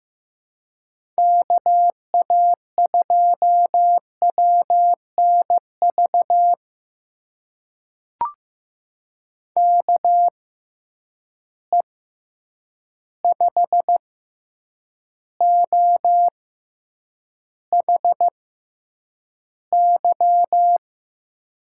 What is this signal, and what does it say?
Morse code, heard as a keyed tone, about 15 wpm, spaced as for 8 wpm, 700 Hz. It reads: KA2WNV KE5OHY
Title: USA Call Signs - 6 Characters - 6x Character Spacing - 15wpm